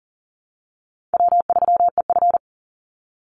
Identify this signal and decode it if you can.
Morse code, heard as a keyed tone, about 40 wpm, 700 Hz. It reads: W3EF
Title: Easy Call Signs Repeated 3x - 40wpm